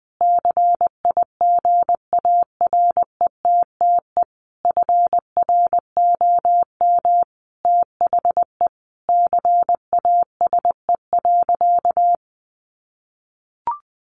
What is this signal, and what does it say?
Morse code, heard as a keyed tone, about 19 wpm, 700 Hz. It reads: CIGARETTE FROM THE CASE.
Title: Sentences from Top 1000 Words 20wpm